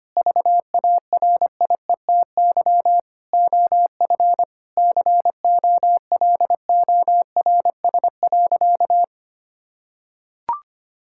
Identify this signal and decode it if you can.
Morse code, heard as a keyed tone, about 25 wpm, 700 Hz. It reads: VARIETY OF COLORS.
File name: practice / 25wpm / Sentences with 4-Word Phrases - 25wpm